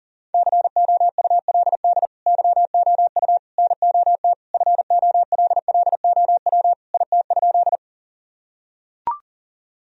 Morse code, heard as a keyed tone, about 40 wpm, 700 Hz. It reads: COULD YOU NOT FOLLOW IT?
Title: Sentences from Top 1000 Words - Review Part 1 - 40wpm